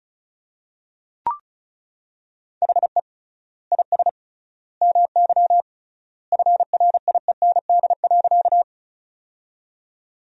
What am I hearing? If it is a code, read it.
Morse code, heard as a keyed tone, about 36 wpm, 700 Hz. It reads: HE IS MY FRIEND.
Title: 3x Word-Spacing - Sentences from Top 300 Words - 35wpm